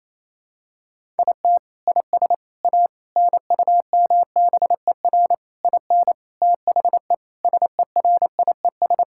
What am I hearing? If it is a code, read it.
Morse code, heard as a keyed tone, about 28 wpm, 700 Hz. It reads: IT IS A NUMBER IN THE SERIES
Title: Sentences from Top 1000 Words - Review Part 5 - 28wpm